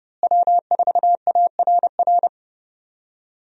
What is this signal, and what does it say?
Morse code, heard as a keyed tone, about 29 wpm, 700 Hz. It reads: W4ARR
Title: USA Call Signs - 5 Characters - Repeated 3x - 30wpm